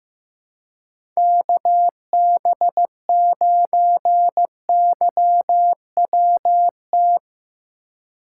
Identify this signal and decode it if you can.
Morse code, heard as a keyed tone, about 15 wpm, 700 Hz. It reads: KB9YWT
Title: USA Call Signs - 4x Character Spacing - 15wpm